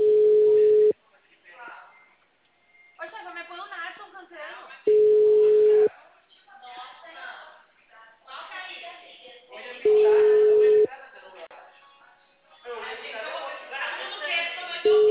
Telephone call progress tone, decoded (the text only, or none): ringback tone